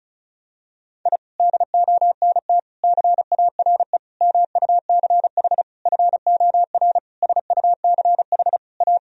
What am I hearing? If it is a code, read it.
Morse code, heard as a keyed tone, about 36 wpm, 700 Hz. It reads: I DONT CARE MUCH FOR SUCH A
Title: Sentences from Top 1000 Words - Review Part 4 - 35wpm